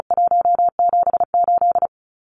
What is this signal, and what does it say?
Morse code, heard as a keyed tone, about 34 wpm, 700 Hz. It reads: 178
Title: Numbers - 4-Digit 35wpm